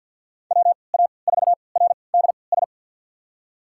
Morse code, heard as a keyed tone, about 47 wpm, spaced as for 28 wpm, 700 Hz. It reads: WA4RDS
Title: USA Call Signs - 3x Character Spacing - 50wpm